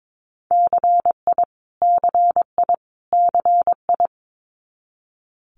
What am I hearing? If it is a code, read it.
Morse code, heard as a keyed tone, about 22 wpm, 700 Hz. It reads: CI CI CI
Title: Multi-letter Phonemes Repeated 3x 22wpm